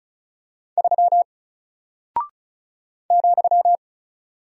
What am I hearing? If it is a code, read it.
Morse code, heard as a keyed tone, about 36 wpm, 700 Hz. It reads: 3 ,
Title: Single Character (Letter:Number:Punctuation:Prosign) 35wpm